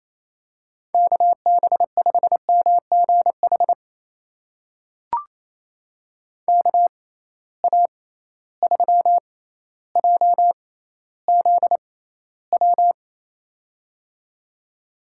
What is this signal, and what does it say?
Morse code, heard as a keyed tone, about 28 wpm, spaced as for 14 wpm, 700 Hz. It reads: KB5MGH KA3JZW
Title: USA Call Signs - 6 Characters - 6x Character Spacing - 28wpm